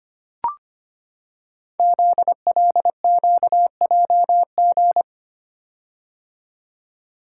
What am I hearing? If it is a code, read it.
Morse code, heard as a keyed tone, about 25 wpm, 700 Hz. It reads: ZLQJG
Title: Code Groups - 5 Letters - 25wpm